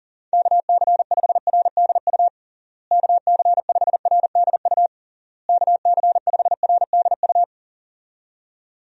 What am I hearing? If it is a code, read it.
Morse code, heard as a keyed tone, about 40 wpm, 700 Hz. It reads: KC5RDU KC5RDU KC5RDU
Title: USA Call Signs - 6 Characters - Repeated 3x - 40wpm